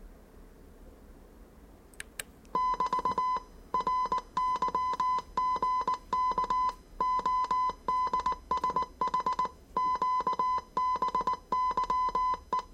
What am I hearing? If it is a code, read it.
Morse code, heard as a keyed tone, about 19 wpm, 1010 Hz. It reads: <BT> RYGK ODSH QBYE
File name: 业余无线电 / ●未分类 / 字码80